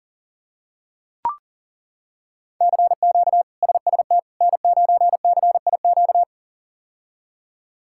Morse code, heard as a keyed tone, about 40 wpm, 700 Hz. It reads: CQ SST N9CIQ
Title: K1USN SST QSO Practice - 40wpm